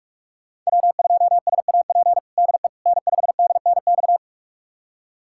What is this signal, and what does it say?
Morse code, heard as a keyed tone, about 42 wpm, 700 Hz. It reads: W2SAP DE N5DNX
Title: USA Callsign DE Callsign - 5 Characters - 45wpm